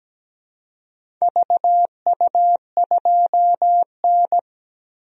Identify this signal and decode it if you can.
Morse code, heard as a keyed tone, about 17 wpm, 700 Hz. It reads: VU2N
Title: DXCC-100 Call Signs - 4 Characters - Repeated 3x - 17wpm